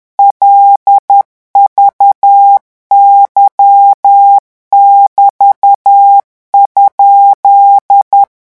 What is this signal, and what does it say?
Morse code, heard as a keyed tone, about 10 wpm, 800 Hz. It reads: LVY<BT>?
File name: morse mp3 / revision13-vitesse 10 mots